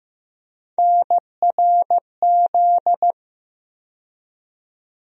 Morse code, heard as a keyed tone, about 15 wpm, 700 Hz. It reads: NRZ
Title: USA Call Sign Postfixes - 3 Letters 15wpm